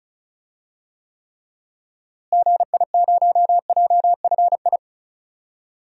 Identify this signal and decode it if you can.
Morse code, heard as a keyed tone, about 35 wpm, 700 Hz. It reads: GI0JFI
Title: DXCC-100 Call Signs - 6 Characters - Repeated 3x - 35wpm